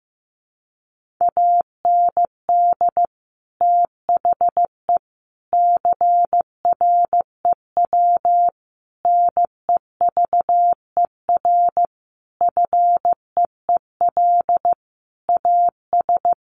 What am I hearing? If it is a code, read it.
Morse code, heard as a keyed tone, about 15 wpm, 700 Hz. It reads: AND THE CREW NEVER FEEL AS
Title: Sentences from Top 1500 Words 15wpm